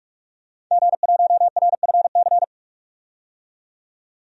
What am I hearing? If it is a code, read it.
Morse code, heard as a keyed tone, about 46 wpm, 700 Hz. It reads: G1RFC
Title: DXCC-100 Call Signs - 5 Characters - 45wpm